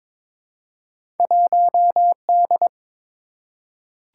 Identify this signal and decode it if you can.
Morse code, heard as a keyed tone, about 22 wpm, 700 Hz. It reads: 1D
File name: practice / 22wpm / Two Letter-Number Combinations - 22wpm